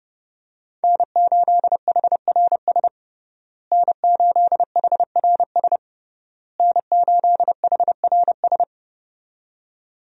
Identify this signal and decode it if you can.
Morse code, heard as a keyed tone, about 30 wpm, 700 Hz. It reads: N8HRS N8HRS N8HRS
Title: USA Call Signs - 5 Characters - Repeated 3x - 30wpm